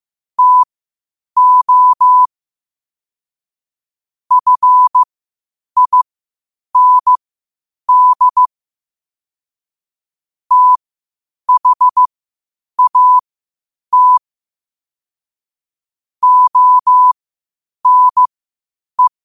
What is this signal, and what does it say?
Morse code, heard as a keyed tone, about 15 wpm, spaced as for 8 wpm, 1000 Hz. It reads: TO FIND THAT ONE
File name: qotdfiles / 8wpm